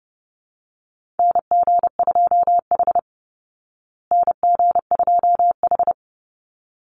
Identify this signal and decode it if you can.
Morse code, heard as a keyed tone, about 30 wpm, 700 Hz. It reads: NG2H NG2H
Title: Call Signs Repeated 3x - 30wpm